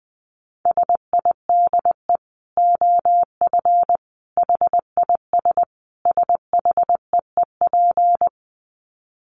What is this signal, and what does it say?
Morse code, heard as a keyed tone, about 20 wpm, 700 Hz. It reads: SIDE OF HIS SHEEP
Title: Dit Challenge 20wpm